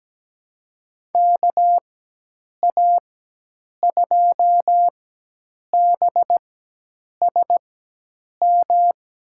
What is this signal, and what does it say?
Morse code, heard as a keyed tone, about 17 wpm, spaced as for 8 wpm, 700 Hz. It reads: KA2BSM
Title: USA Call Signs - 6 Characters - 4x Character Spacing - 17wpm